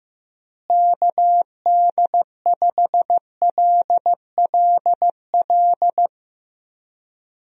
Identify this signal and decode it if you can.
Morse code, heard as a keyed tone, about 15 wpm, 700 Hz. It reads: KD5LLL
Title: USA Call Signs - 6 Characters - Repeated 3x - 15wpm